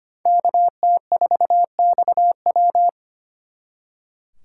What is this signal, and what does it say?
Morse code, heard as a keyed tone, about 32 wpm, 700 Hz. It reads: KT4XW